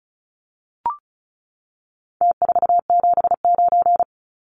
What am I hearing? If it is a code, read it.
Morse code, heard as a keyed tone, about 34 wpm, 700 Hz. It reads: T479
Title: Numbers - Contest Style - 4-Digit 35wpm